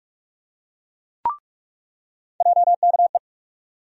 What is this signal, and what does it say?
Morse code, heard as a keyed tone, about 46 wpm, 700 Hz. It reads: JKE
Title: USA Call Sign Postfixes - 3 Letters 45wpm